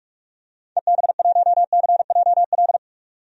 Morse code, heard as a keyed tone, about 41 wpm, 700 Hz. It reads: EB1CJL
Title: DXCC-100 Call Signs - 6 Characters - 45wpm